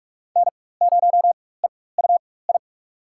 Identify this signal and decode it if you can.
Morse code, heard as a keyed tone, about 42 wpm, spaced as for 20 wpm, 700 Hz. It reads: N0EUI
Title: USA Call Signs - 5 Characters - 4x Character Spacing - 45wpm